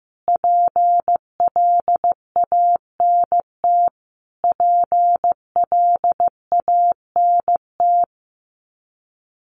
Morse code, heard as a keyed tone, about 15 wpm, 700 Hz. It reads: PLANT PLANT
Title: Top 300 Words Repeated 15wpm